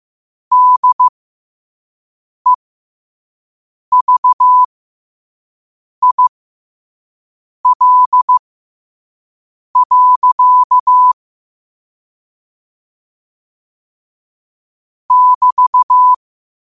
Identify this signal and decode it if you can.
Morse code, heard as a keyed tone, about 15 wpm, spaced as for 5 wpm, 1000 Hz. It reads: DEVIL. <BT>
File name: qotdfiles / 5wpm